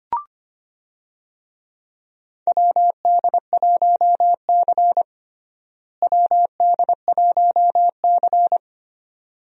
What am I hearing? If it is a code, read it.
Morse code, heard as a keyed tone, about 25 wpm, 700 Hz. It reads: WD1C WD1C
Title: USA Call Signs - 4 Characters - Repeated 2x - 25wpm